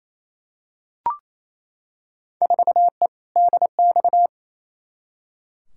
Morse code, heard as a keyed tone, about 28 wpm, 700 Hz. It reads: 4E DX
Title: Field Day Exchanges Encore - 28wpm